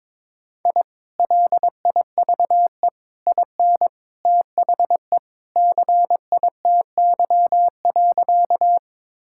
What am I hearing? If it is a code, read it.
Morse code, heard as a keyed tone, about 22 wpm, 700 Hz. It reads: I LIVE IN THE CITY.
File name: practice / 22wpm / Sentences with 2-Word Phrases - 22wpm